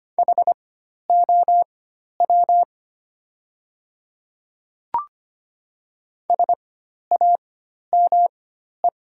Morse code, heard as a keyed tone, about 25 wpm, spaced as for 12 wpm, 700 Hz. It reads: HOW SAME
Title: 4x Character Spacing - Top 200 Words - 25wpm